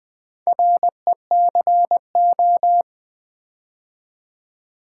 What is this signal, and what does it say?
Morse code, heard as a keyed tone, about 20 wpm, 700 Hz. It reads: RECO
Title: Build A Word - 4 Letters At A Time - 8-Letter Words - 20wpm